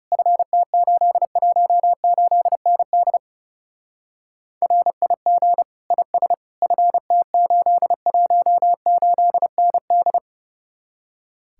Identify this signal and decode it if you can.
Morse code, heard as a keyed tone, about 34 wpm, 700 Hz. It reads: FT818ND RIG IS FT818ND
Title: Speed-Racing Common QSO Phrases - 30wpm